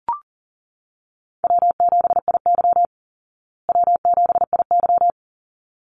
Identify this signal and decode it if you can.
Morse code, heard as a keyed tone, about 40 wpm, 700 Hz. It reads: W7IY W7IY
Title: Easy Call Signs Repeated 2x - 40wpm